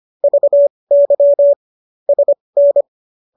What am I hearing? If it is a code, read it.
Morse code, heard as a keyed tone, about 25 wpm, spaced as for 20 wpm, 570 Hz. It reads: VY SN